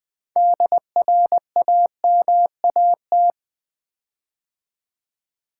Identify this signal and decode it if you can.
Morse code, heard as a keyed tone, about 20 wpm, 700 Hz. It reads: DRAMAT
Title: Build A Word - 6 Letters At A Time - 12-Letter Words - 20wpm